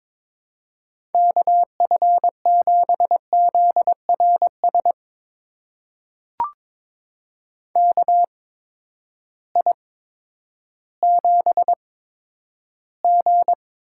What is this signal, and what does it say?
Morse code, heard as a keyed tone, about 22 wpm, spaced as for 11 wpm, 700 Hz. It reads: KF7ZRS KI7G
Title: USA Call Signs - 6 Characters - 8x Character Spacing - 22wpm